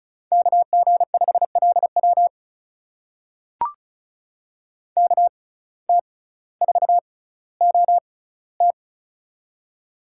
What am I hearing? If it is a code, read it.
Morse code, heard as a keyed tone, about 36 wpm, spaced as for 18 wpm, 700 Hz. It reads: KG5LW KT4OT
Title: USA Call Signs - 5 Characters - 6x Character Spacing - 35wpm